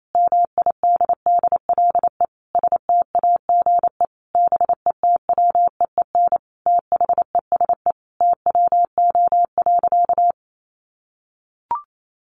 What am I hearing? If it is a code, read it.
Morse code, heard as a keyed tone, about 28 wpm, 700 Hz. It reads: MIDDLE STAGE BETWEEN THESE TWO.
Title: Sentences from Top 1200 Words 28wpm